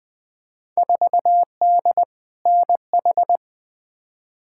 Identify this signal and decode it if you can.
Morse code, heard as a keyed tone, about 20 wpm, 700 Hz. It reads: 4D NH
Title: Field Day Exchanges Encore - 20wpm